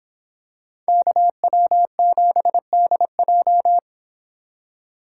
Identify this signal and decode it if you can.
Morse code, heard as a keyed tone, about 26 wpm, 700 Hz. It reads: KW7DJ